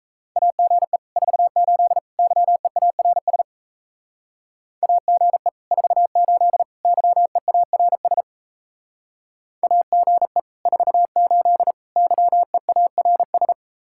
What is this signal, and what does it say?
Morse code, heard as a keyed tone, about 36 wpm, 700 Hz. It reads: AGE 48 YEARS AGE 48 YEARS AGE 48 YEARS